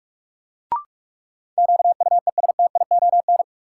Morse code, heard as a keyed tone, about 46 wpm, 700 Hz. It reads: QUESTION
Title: Sets of 5 Words Encore - New Word Drill - 45wpm